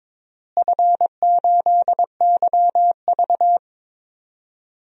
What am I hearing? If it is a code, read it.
Morse code, heard as a keyed tone, about 22 wpm, 700 Hz. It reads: F8YV
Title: DXCC-100 Call Signs - 4 Characters - Repeated 3x - 22wpm